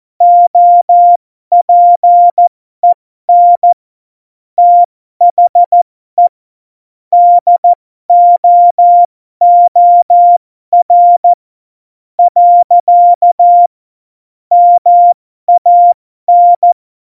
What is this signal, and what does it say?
Morse code, heard as a keyed tone, about 14 wpm, spaced as for 11 wpm, 700 Hz. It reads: OPEN THE DOOR. MAN
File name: Morsekurs / Witze